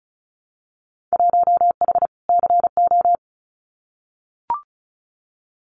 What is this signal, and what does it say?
Morse code, heard as a keyed tone, about 34 wpm, 700 Hz. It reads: 1H CO
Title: Winter Field Day Exchanges 35wpm